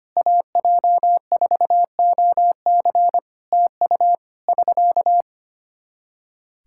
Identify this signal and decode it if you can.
Morse code, heard as a keyed tone, about 25 wpm, 700 Hz. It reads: AJ4OC TU <SK>
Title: Practice QSO - Exchange 4 - Ending - 25wpm